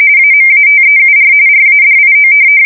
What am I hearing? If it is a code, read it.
RTTY, about 45.45 baud. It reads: AMATEUR RADIO
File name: fsk170 / teletype-RTTY